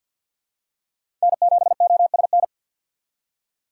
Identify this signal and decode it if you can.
Morse code, heard as a keyed tone, about 53 wpm, 700 Hz. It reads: N7OSN
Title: USA Call Signs - 4x Character Spacing - 50wpm